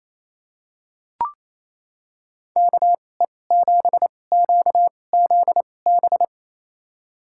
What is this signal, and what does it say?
Morse code, heard as a keyed tone, about 28 wpm, spaced as for 20 wpm, 700 Hz. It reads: KE7QZB
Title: USA Call Signs - 6 Characters - 2x Character Spacing - 28wpm